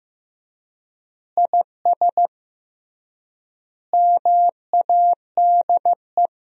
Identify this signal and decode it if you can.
Morse code, heard as a keyed tone, about 15 wpm, 700 Hz. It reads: IS MADE
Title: 3x Word-Spacing - Sentences from Top 500 Words - 15wpm